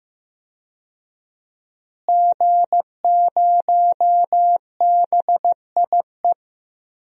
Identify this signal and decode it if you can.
Morse code, heard as a keyed tone, about 15 wpm, 700 Hz. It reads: G0BIE
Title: DXCC-100 Call Signs - 5 Characters - Repeated 3x - 15wpm